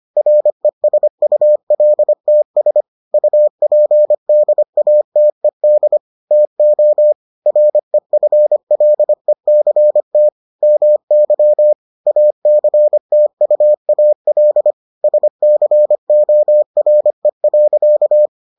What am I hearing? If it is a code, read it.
Morse code, heard as a keyed tone, about 25 wpm, 600 Hz. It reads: RESULTS UPDATED TO REFLECT MY ACTUAL SCORE.